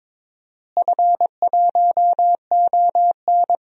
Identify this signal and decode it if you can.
Morse code, heard as a keyed tone, about 22 wpm, 700 Hz. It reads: F1ON
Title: DXCC-100 Call Signs - 4 Characters - Repeated 3x - 22wpm